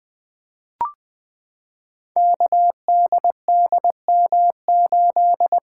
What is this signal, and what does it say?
Morse code, heard as a keyed tone, about 20 wpm, 700 Hz. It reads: KDDM8
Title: Code Groups - 5 Letter:Numbers - 20wpm